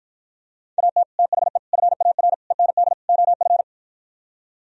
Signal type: Morse code

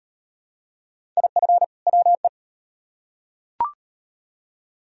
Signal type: Morse code

